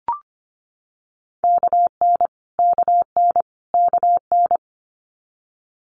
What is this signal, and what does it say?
Morse code, heard as a keyed tone, about 25 wpm, 700 Hz. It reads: KN KN KN
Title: Multi-letter Phonemes Repeated 3x 25wpm